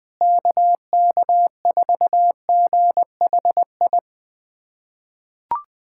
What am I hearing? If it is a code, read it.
Morse code, heard as a keyed tone, about 19 wpm, 700 Hz. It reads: KK4GHI
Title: USA Callsign DE Callsign 20wpm